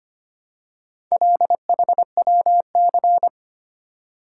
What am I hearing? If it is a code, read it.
Morse code, heard as a keyed tone, about 25 wpm, 700 Hz. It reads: LHWC
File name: practice / 25wpm / Four Letter-Number Combinations - 25wpm